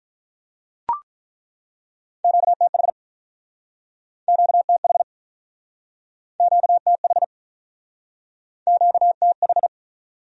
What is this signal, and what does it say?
Morse code, heard as a keyed tone, about 48 wpm, 700 Hz. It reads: QTH QTH QTH QTH